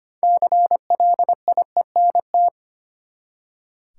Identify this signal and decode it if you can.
Morse code, heard as a keyed tone, about 25 wpm, 700 Hz. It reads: CLIENT
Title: Six Letter Words - 25wpm